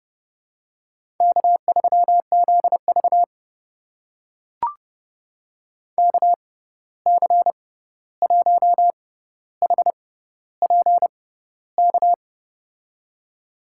Morse code, heard as a keyed tone, about 30 wpm, spaced as for 14 wpm, 700 Hz. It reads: K3ZV KC1HPK
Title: USA Call Signs - 6x Character Spacing - 30wpm